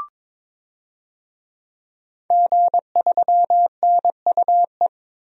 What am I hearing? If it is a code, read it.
Morse code, heard as a keyed tone, about 22 wpm, 700 Hz. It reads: G3NUE
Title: DXCC-100 Call Signs - 5 Characters - 22wpm